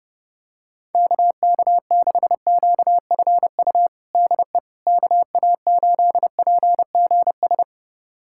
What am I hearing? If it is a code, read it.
Morse code, heard as a keyed tone, about 32 wpm, 700 Hz. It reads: KK6QFU DE KA8PGS
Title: USA Callsign DE Callsign - 6 Characters - 30wpm